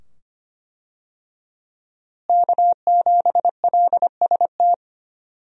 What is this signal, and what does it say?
Morse code, heard as a keyed tone, about 25 wpm, 700 Hz. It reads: K7LST
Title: USA Call Signs - 5 Characters - Repeated 2x - 25wpm